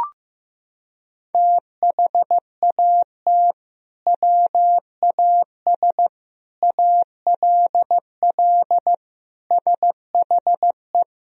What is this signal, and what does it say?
Morse code, heard as a keyed tone, about 15 wpm, 700 Hz. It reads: THAT WAS ALL SHE
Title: Sentences from Top 100 Words and Abbreviations 15wpm